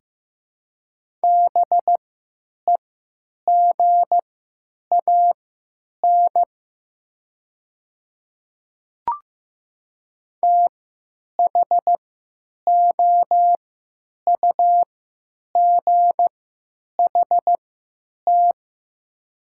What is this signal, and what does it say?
Morse code, heard as a keyed tone, about 15 wpm, spaced as for 9 wpm, 700 Hz. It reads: BEGAN THOUGHT
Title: 3x Character Spacing - Top 300 Words - 15wpm